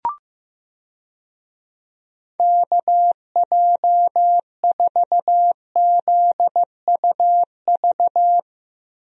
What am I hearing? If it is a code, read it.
Morse code, heard as a keyed tone, about 15 wpm, 700 Hz. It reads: KJ4ZUV
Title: USA Call Signs - 6 Characters - Repeated 2x - 15wpm